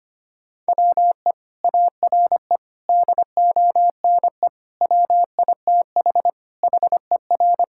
Morse code, heard as a keyed tone, about 25 wpm, 700 Hz. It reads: WE ARE DONE WITH HER
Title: Sentences from Top 1600 Words 25wpm